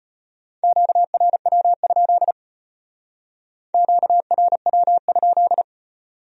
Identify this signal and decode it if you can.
Morse code, heard as a keyed tone, about 31 wpm, 700 Hz. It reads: QRW? QRW?
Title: Speed-Racing Q-Codes - 25wpm